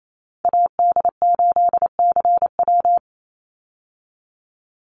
Morse code, heard as a keyed tone, about 28 wpm, 700 Hz. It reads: AD8CW
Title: Call Signs Repeated 3x - 28wpm